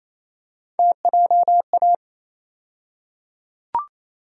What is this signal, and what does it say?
Morse code, heard as a keyed tone, about 28 wpm, 700 Hz. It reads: TJA
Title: USA Call Sign Postfixes - 3 Letters 28wpm